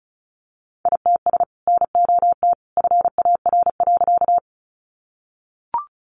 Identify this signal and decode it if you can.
Morse code, heard as a keyed tone, about 34 wpm, 700 Hz. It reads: ITS NOT FAR.
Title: Sentences from Top 600 Words 35wpm